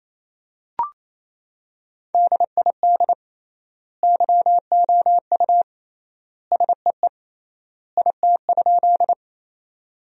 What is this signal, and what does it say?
Morse code, heard as a keyed tone, about 28 wpm, 700 Hz. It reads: DID YOU SEE IT?
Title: 3x Word-Spacing - Sentences from Top 100 Words - 28wpm